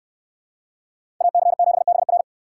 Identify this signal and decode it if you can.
Morse code, heard as a keyed tone, about 69 wpm, 700 Hz. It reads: AC7BR